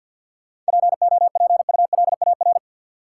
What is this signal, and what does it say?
Morse code, heard as a keyed tone, about 53 wpm, 700 Hz. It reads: POPULAR
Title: Sentences from Top 1000 Words - Review Part 3 - New Word Drill - 50wpm